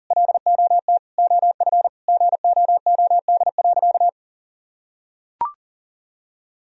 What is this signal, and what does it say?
Morse code, heard as a keyed tone, about 40 wpm, 700 Hz. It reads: LOT OF GOOD.
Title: Sentences from Top 1000 Words - Review Part 3 - 40wpm